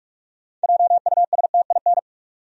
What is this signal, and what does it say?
Morse code, heard as a keyed tone, about 46 wpm, 700 Hz. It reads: JUSTIN
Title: USA Common Names - 45wpm